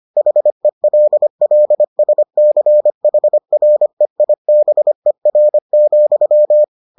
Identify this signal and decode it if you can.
Morse code, heard as a keyed tone, about 25 wpm, 600 Hz. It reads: HELLSCHREIBER,